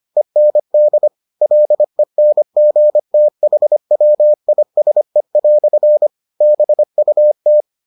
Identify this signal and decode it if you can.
Morse code, heard as a keyed tone, about 25 wpm, 600 Hz. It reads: END LENGTHWISE" BUT